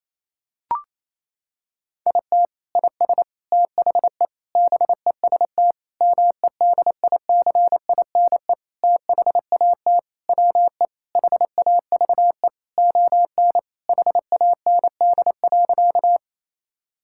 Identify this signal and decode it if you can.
Morse code, heard as a keyed tone, about 28 wpm, 700 Hz. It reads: IT IS THE BEST MEDICINE THAT WE HAVE ON HAND.
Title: Sentences from Top 1000 Words - Review Part 6 - 28wpm